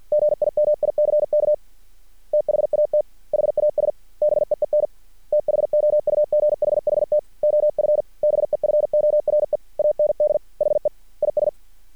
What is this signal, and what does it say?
Morse code, heard as a keyed tone, about 50 wpm, 600 Hz. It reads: GIMICK THAT HAS BEEN THOUGHHT OF BEFORE AND HE IS
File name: archived / CRSnet